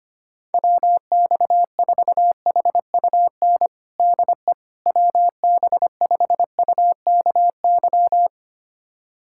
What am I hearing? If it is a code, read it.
Morse code, heard as a keyed tone, about 25 wpm, 700 Hz. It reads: WX4HUN DE WB5UKY
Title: USA Callsign DE Callsign - 6 Characters - 25wpm